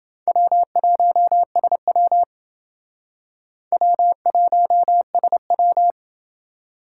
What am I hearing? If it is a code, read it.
Morse code, heard as a keyed tone, about 27 wpm, 700 Hz. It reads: W1SW W1SW